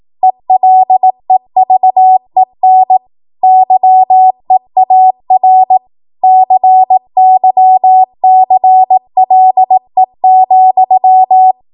Morse code, heard as a keyed tone, about 18 wpm, 750 Hz. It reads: ELEVEN YEAR CYCLE,
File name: Morse / 18WPM